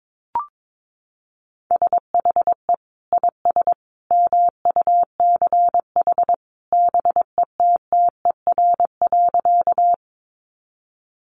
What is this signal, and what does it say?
Morse code, heard as a keyed tone, about 22 wpm, 700 Hz. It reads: SHE IS MUCH BETTER.
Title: Sentences from Top 400 Words 22wpm